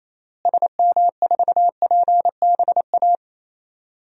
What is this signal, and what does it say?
Morse code, heard as a keyed tone, about 28 wpm, 700 Hz. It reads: SM4PBA
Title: DXCC-100 Call Signs - 6 Characters - Repeated 3x - 28wpm